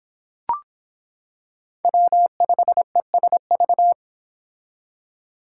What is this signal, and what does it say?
Morse code, heard as a keyed tone, about 25 wpm, 700 Hz. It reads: W5ESV